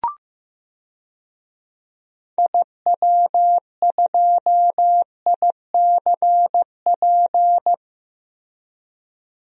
Morse code, heard as a keyed tone, about 15 wpm, 700 Hz. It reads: IW2ICP
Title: DXCC-100 Call Signs - 6 Characters - Repeated 2x - 15wpm